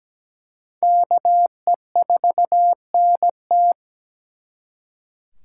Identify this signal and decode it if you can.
Morse code, heard as a keyed tone, about 20 wpm, 700 Hz. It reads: KE4NT
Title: Speed Racing - USA 2x2 Call Signs 17wpm